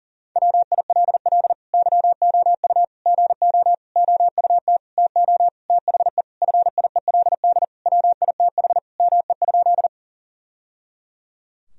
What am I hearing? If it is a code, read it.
Morse code, heard as a keyed tone, about 40 wpm, 700 Hz. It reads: WILL YOU GO OUT TO THE FIELD WITH ME?
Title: Sentences from Top 1000 Words - Review Part 2 - 40wpm